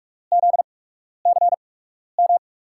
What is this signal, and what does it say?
Morse code, heard as a keyed tone, about 46 wpm, spaced as for 12 wpm, 700 Hz. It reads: ZCM
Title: USA Call Signs - 5 Characters - 8x Character Spacing - 45wpm